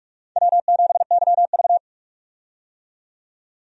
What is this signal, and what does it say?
Morse code, heard as a keyed tone, about 44 wpm, 700 Hz. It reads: W7YV
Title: USA Call Signs - 4 Characters - 45wpm